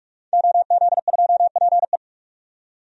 Morse code, heard as a keyed tone, about 46 wpm, 700 Hz. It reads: OZ2PE
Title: DXCC-100 Call Signs - 5 Characters - 45wpm